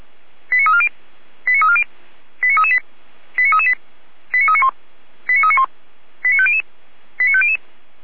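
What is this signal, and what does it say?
Five-tone selective call (ZVEI), heard as quick runs of five tones, tones of about 70 ms. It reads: 89340 89340 89338 89338 89391 89391 89500 89500
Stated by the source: ZVEI five-tone selective call 89315 received by SDR.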